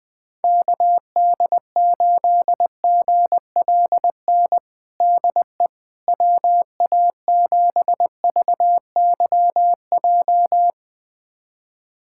Morse code, heard as a keyed tone, about 21 wpm, 700 Hz. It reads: KD8GLN DE WA7VYJ
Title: USA Callsign DE Callsign 20wpm